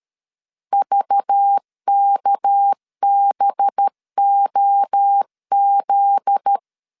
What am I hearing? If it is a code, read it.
Morse code, heard as a keyed tone, about 13 wpm, 800 Hz. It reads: VKBOZ
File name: LEZIONI / 8 YJ45 / 8 Y 12wpm 2228